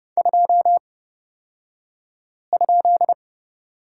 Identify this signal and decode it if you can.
Morse code, heard as a keyed tone, about 29 wpm, spaced as for 5 wpm, 690 Hz. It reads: 2?
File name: Lesson 36d - BT - Two Characters 30wpm